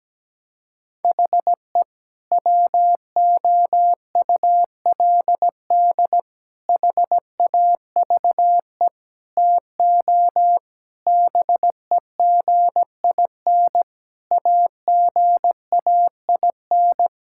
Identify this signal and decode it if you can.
Morse code, heard as a keyed tone, about 17 wpm, 700 Hz. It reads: HE WOULD HAVE TO BEGIN AGAIN
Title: Sentences from Top 1000 Words - Review Part 5 - 17wpm